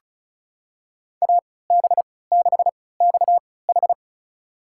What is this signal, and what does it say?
Morse code, heard as a keyed tone, about 36 wpm, spaced as for 20 wpm, 700 Hz. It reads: AB6XH
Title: USA Call Signs - 3x Character Spacing - 35wpm